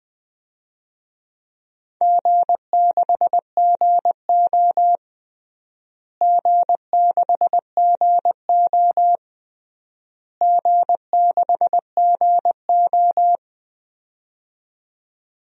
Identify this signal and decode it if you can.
Morse code, heard as a keyed tone, about 20 wpm, 700 Hz. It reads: G6GO G6GO G6GO
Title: DXCC-100 Call Signs - 4 Characters - Repeated 3x - 20wpm